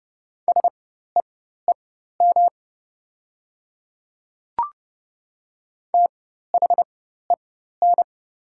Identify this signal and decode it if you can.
Morse code, heard as a keyed tone, about 30 wpm, spaced as for 14 wpm, 690 Hz. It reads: SEEM THEN
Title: 4x Character Spacing - Top 500 Words - 30wpm